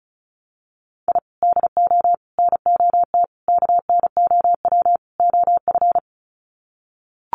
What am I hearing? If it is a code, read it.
Morse code, heard as a keyed tone, about 34 wpm, 700 Hz. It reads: I DO NOT KNOW OF
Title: Sets of 5 Words Encore2 35wpm